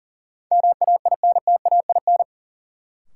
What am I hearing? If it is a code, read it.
Morse code, heard as a keyed tone, about 40 wpm, 700 Hz. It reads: MAINTAIN
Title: Sentences from Top 1000 Words - Review Part 5 - New Word Drill - 40wpm